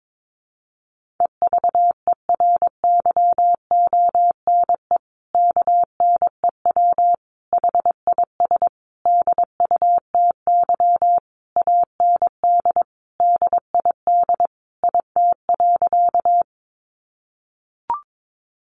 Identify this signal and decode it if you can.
Morse code, heard as a keyed tone, about 22 wpm, 700 Hz. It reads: EVERYONE KNEW HIS DUTY AND DID IT.
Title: Sentences from Top 1500 Words 22wpm